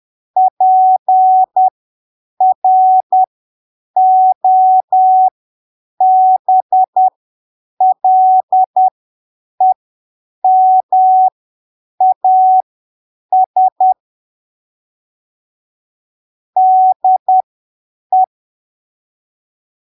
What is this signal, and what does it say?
Morse code, heard as a keyed tone, about 10 wpm, spaced as for 7 wpm, 750 Hz. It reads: PROBLEMAS DE